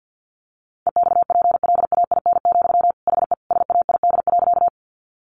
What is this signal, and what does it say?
Morse code, heard as a keyed tone, about 50 wpm, 700 Hz. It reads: EXPLAIN, HE SAID.
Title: Sentences from Top 1100 Words 50wpm